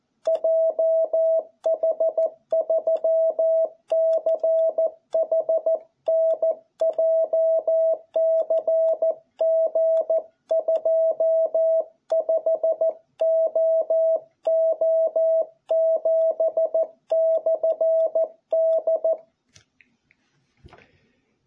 Morse code, heard as a keyed tone, about 14 wpm, 660 Hz. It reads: JH3CHN JCG25OO7/D